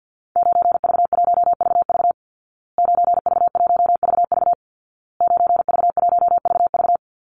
Morse code, heard as a keyed tone, about 50 wpm, 700 Hz. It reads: 9V1VV 9V1VV 9V1VV
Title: Call Signs Repeated 2x - 50wpm